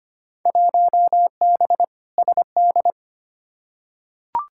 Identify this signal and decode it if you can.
Morse code, heard as a keyed tone, about 25 wpm, 700 Hz. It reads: 1B SD
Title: Field Day Exchanges Encore - 25wpm